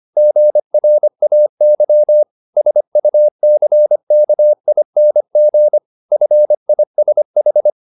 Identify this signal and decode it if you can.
Morse code, heard as a keyed tone, about 25 wpm, 600 Hz. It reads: GRAY SUCKING FISH